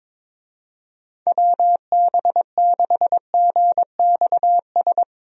Morse code, heard as a keyed tone, about 22 wpm, 700 Hz. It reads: WB6GXS
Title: USA Call Signs - 6 Characters - 22wpm